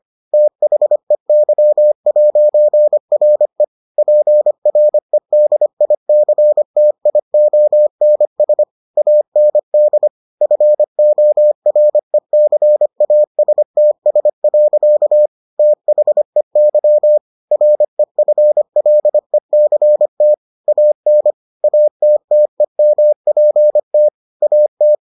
Morse code, heard as a keyed tone, about 25 wpm, 600 Hz. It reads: THEY'RE PREDICTIONS AND FORECASTS. THEY REFLECT AN ATTEMPT AT